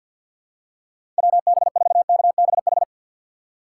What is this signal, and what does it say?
Morse code, heard as a keyed tone, about 53 wpm, 700 Hz. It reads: WB4KBH
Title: USA Call Signs - 6 Characters - 4x Character Spacing - 50wpm